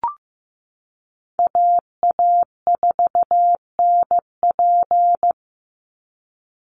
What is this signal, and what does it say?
Morse code, heard as a keyed tone, about 15 wpm, 700 Hz. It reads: AA4NP
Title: Call Signs Repeated 2x - 15wpm